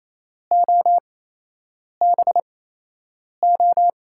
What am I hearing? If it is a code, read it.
Morse code, heard as a keyed tone, about 28 wpm, spaced as for 8 wpm, 700 Hz. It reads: OBO
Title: USA Call Signs - 8x Character Spacing - 28wpm